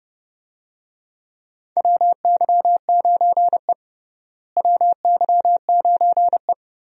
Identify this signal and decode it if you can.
Morse code, heard as a keyed tone, about 30 wpm, 700 Hz. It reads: WY9E WY9E
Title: USA Call Signs - 4 Characters - Repeated 2x - 30wpm